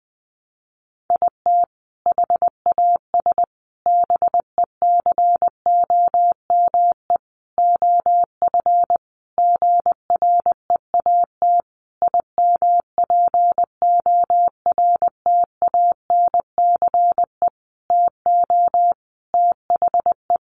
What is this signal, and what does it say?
Morse code, heard as a keyed tone, about 20 wpm, 700 Hz. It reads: IT HAS BECOME OF GREAT IMPORTANCE TO THE
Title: Sentences from Top 1200 Words 20wpm